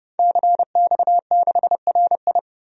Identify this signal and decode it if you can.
Morse code, heard as a keyed tone, about 30 wpm, 700 Hz. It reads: CX6RI
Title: DXCC-100 Call Signs - 5 Characters - 30wpm